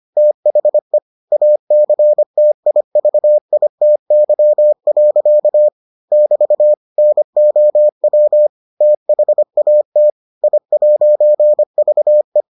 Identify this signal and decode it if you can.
Morse code, heard as a keyed tone, about 25 wpm, 600 Hz. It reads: THE ACTIVITY. <BT> NOW THAT I'VE